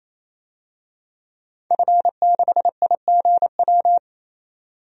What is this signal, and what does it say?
Morse code, heard as a keyed tone, about 28 wpm, 700 Hz. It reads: F6IGW
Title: DXCC-100 Call Signs - 5 Characters - 28wpm